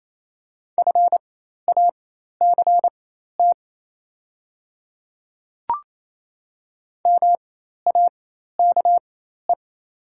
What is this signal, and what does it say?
Morse code, heard as a keyed tone, about 27 wpm, spaced as for 13 wpm, 700 Hz. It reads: FACT MAKE
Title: 4x Character Spacing - Top 500 Words - 28wpm